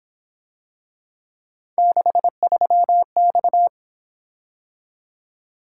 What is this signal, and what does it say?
Morse code, heard as a keyed tone, about 26 wpm, 700 Hz. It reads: 63X
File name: Speed-Racing - Three Letter-Number Combinations - 22wpm